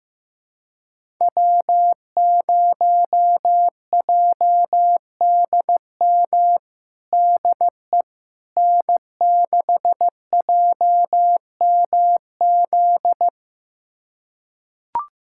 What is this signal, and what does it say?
Morse code, heard as a keyed tone, about 16 wpm, 700 Hz. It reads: W0JDM DE N6JMZ
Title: USA Callsign DE Callsign 15wpm